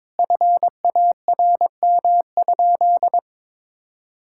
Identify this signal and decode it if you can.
Morse code, heard as a keyed tone, about 23 wpm, 700 Hz. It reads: FARM?
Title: Sentences from Top 1000 Words - Review Part 6 - 22wpm